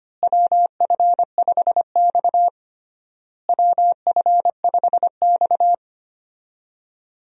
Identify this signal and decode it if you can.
Morse code, heard as a keyed tone, about 25 wpm, 700 Hz. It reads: WF5X WF5X
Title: USA Call Signs - 4 Characters - Repeated 3x - 25wpm